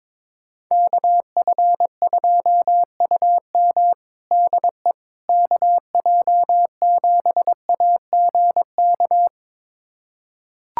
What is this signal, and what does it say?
Morse code, heard as a keyed tone, about 22 wpm, 700 Hz. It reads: KF2UM DE KJ7AGK
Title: USA Callsign DE Callsign 22wpm